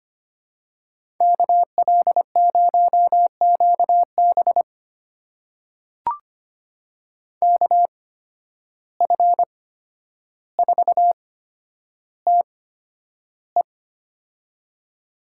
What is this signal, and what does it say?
Morse code, heard as a keyed tone, about 25 wpm, spaced as for 11 wpm, 700 Hz. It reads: KL0QB KF4TE
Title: USA Call Signs - 5 Characters - 8x Character Spacing - 25wpm